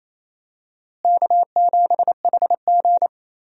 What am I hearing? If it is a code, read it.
Morse code, heard as a keyed tone, about 28 wpm, 700 Hz. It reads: K7HG
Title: USA Call Signs - 4 Characters - Repeated 2x - 28wpm